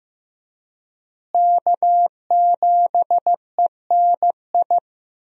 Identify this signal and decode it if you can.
Morse code, heard as a keyed tone, about 15 wpm, 700 Hz. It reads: K7ENI
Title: USA Call Signs - 5 Characters - Repeated 2x - 15wpm